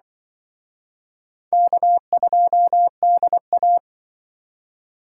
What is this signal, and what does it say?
Morse code, heard as a keyed tone, about 23 wpm, 700 Hz. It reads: K2DA
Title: Speed Racing - USA 1x2 Call Signs 20wpm